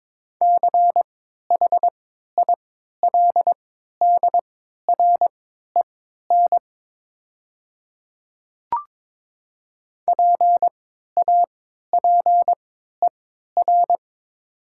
Morse code, heard as a keyed tone, about 22 wpm, spaced as for 12 wpm, 700 Hz. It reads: CHILDREN PAPER
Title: 3x Character Spacing - Top 300 Words - 22wpm